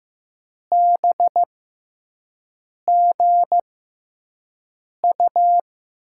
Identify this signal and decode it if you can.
Morse code, heard as a keyed tone, about 15 wpm, spaced as for 5 wpm, 700 Hz. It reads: BGU
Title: USA Call Signs - 6 Characters - 6x Character Spacing - 15wpm